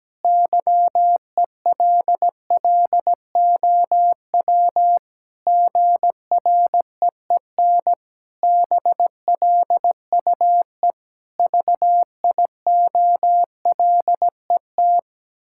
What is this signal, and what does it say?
Morse code, heard as a keyed tone, about 17 wpm, 700 Hz. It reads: YELLOW GREEN BLUE VIOLET
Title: Pangram Challenge v2 - 17wpm